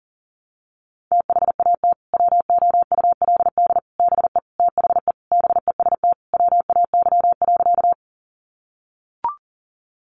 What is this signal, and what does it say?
Morse code, heard as a keyed tone, about 40 wpm, 700 Hz. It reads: THAT WOULD BE THE BEST WAY.
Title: Sentences from Top 400 Words 40wpm